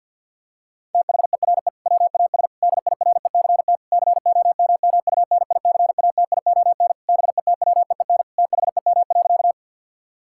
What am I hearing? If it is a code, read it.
Morse code, heard as a keyed tone, about 50 wpm, 700 Hz. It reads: THERE WAS DIRECT COMMUNICATION BETWEEN THEM.
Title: Sentences from Top 1000 Words - Review Part 6 - 50wpm